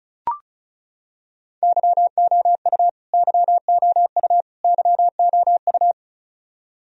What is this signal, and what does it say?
Morse code, heard as a keyed tone, about 34 wpm, 700 Hz. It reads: YOU YOU YOU
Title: Top 100 Words Repeated 35wpm